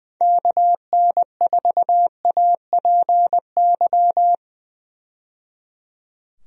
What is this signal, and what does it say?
Morse code, heard as a keyed tone, about 20 wpm, 700 Hz. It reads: KN4APY
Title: USA Call Signs - 6 Characters - 20wpm